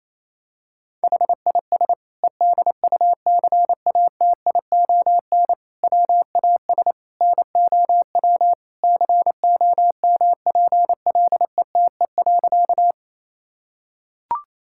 Morse code, heard as a keyed tone, about 28 wpm, 700 Hz. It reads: HIS EDUCATION WAS NOW COMPLETE.
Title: Sentences from Top 1000 Words - Review Part 6 - 28wpm